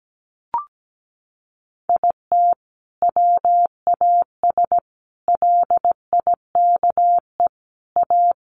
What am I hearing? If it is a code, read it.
Morse code, heard as a keyed tone, about 17 wpm, 700 Hz. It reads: IT WAS LIKE A
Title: Sentences from Top 1400 Words 17wpm